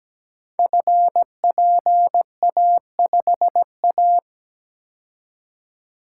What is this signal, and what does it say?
Morse code, heard as a keyed tone, about 17 wpm, 700 Hz. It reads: FPA5A
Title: Code Groups - 5 Letter:Numbers - 17wpm